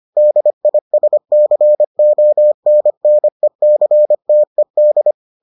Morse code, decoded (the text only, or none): DISCONNECTED